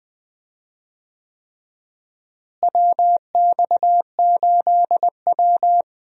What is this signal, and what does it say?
Morse code, heard as a keyed tone, about 20 wpm, 700 Hz. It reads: WX8W
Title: USA Call Signs - 4 Characters - Repeated 3x - 20wpm